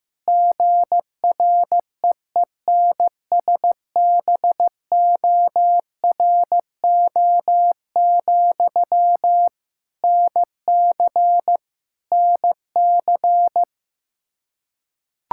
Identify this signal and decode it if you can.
Morse code, heard as a keyed tone, about 15 wpm, 700 Hz. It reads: GREENSBORO, NC NC
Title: Practice QSO - Exchange 1 - Beginning - 15wpm